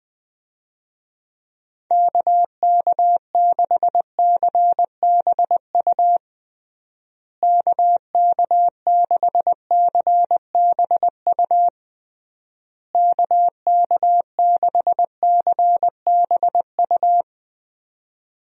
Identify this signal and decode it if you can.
Morse code, heard as a keyed tone, about 21 wpm, 700 Hz. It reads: KK6CBU KK6CBU KK6CBU
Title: USA Call Signs - 6 Characters - Repeated 3x - 20wpm